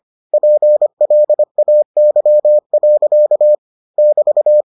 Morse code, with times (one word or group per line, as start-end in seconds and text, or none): 0.3-3.5 PLAY.
4.0-4.6 <BT>